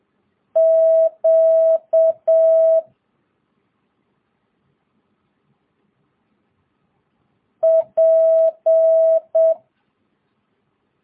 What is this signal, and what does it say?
Morse code, heard as a keyed tone, about 7 wpm, 650 Hz. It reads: Q P
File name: jh8cbh / 2022cw